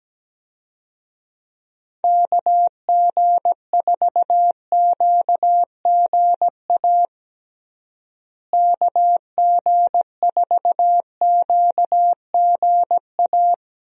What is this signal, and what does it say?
Morse code, heard as a keyed tone, about 17 wpm, 700 Hz. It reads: KG4QGA KG4QGA
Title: USA Call Signs - 6 Characters - Repeated 3x - 17wpm